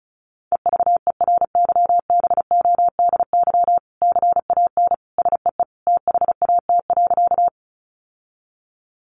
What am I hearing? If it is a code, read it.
Morse code, heard as a keyed tone, about 34 wpm, 700 Hz. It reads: EVERYBODY CAN SEE THAT.
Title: Sentences from Top 1400 Words 35wpm